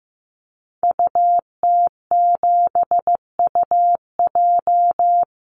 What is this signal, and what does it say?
Morse code, heard as a keyed tone, about 15 wpm, 700 Hz. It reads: UT7UJ
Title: Call Signs Repeated 2x - 15wpm